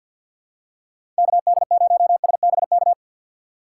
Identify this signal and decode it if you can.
Morse code, heard as a keyed tone, about 47 wpm, 700 Hz. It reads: KD0SBK
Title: USA Call Signs - 6 Characters - 6x Character Spacing - 50wpm